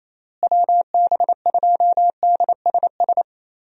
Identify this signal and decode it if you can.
Morse code, heard as a keyed tone, about 27 wpm, 700 Hz. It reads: WB2DSS
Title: USA Call Signs - 6 Characters - Repeated 2x - 28wpm